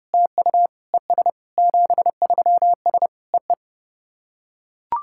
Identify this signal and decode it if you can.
Morse code, heard as a keyed tone, about 30 wpm, 700 Hz. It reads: TU ES 73S EE
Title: SOTA QSO Practice 30wpm